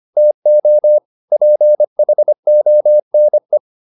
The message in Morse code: TO PHONE

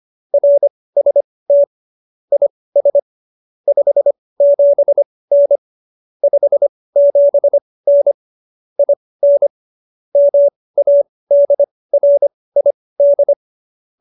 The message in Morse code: RST IS 57N 57N IN MADRID

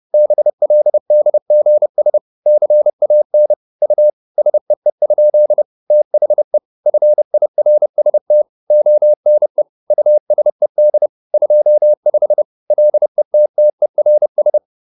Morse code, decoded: BLDGS CAN U SEE? THE FIRST ONE USED 25 LETTERS

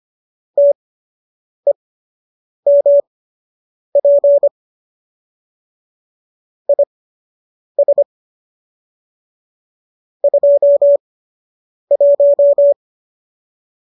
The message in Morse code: TEMP IS 21

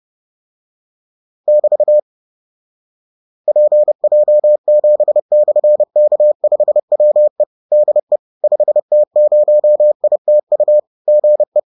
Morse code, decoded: <BT> PJ7/K5WE DE 5T0ITU GE

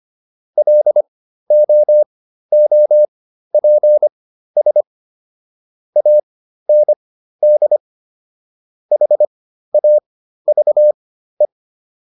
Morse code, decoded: LOOPS AND HAVE

